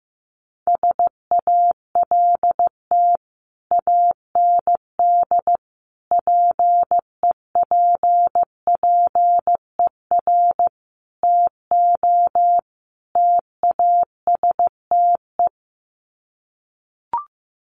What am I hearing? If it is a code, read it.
Morse code, heard as a keyed tone, about 15 wpm, 700 Hz. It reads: SALT AND PEPPER TO TASTE
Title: Sets of 5 Words 15wpm